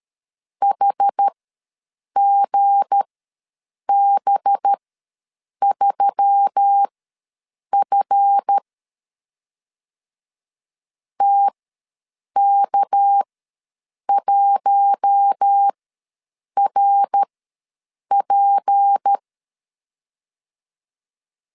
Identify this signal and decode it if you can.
Morse code, heard as a keyed tone, about 13 wpm, spaced as for 7 wpm, 800 Hz. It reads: HGB3F TK1RP
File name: LEZIONI / 7 GF9Z / 7 GF ws 12wpm 3140